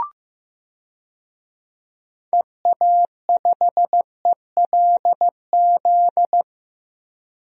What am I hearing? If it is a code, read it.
Morse code, heard as a keyed tone, about 16 wpm, 700 Hz. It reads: EA5ELZ
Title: DXCC-100 Call Signs - 6 Characters - Repeated 3x - 15wpm